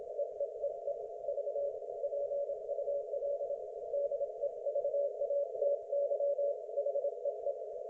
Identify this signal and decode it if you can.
Morse code, heard as a keyed tone, about 26 wpm, 570 Hz. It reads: EEEE V PUL DEVATY HEE
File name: Archiv / CUC216 / ok1fhi cul216